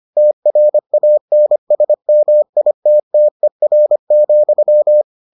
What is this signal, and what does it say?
Morse code, heard as a keyed tone, about 25 wpm, 600 Hz. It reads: TRANSMITTER,